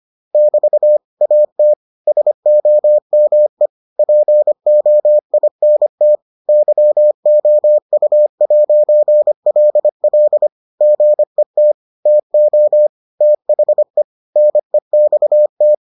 Morse code, heard as a keyed tone, about 25 wpm, 600 Hz. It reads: <BT> AT SOME POINT YOU'LL GET TO THE NEXT